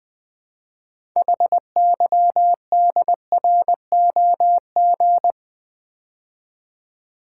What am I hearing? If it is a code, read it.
Morse code, heard as a keyed tone, about 20 wpm, 700 Hz. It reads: HYDROG
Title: Build A Word - 3 Letters At A Time - 8-Letter Words - 20wpm